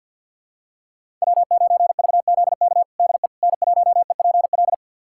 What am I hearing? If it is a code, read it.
Morse code, heard as a keyed tone, about 51 wpm, 700 Hz. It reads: W9VZK DE N1EPL
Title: USA Callsign DE Callsign - 5 Characters - 50wpm